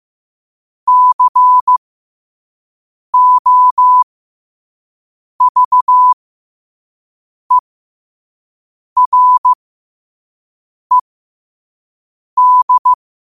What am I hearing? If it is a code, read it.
Morse code, heard as a keyed tone, about 15 wpm, spaced as for 5 wpm, 1000 Hz. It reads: COVERED
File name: qotdfiles / 5wpm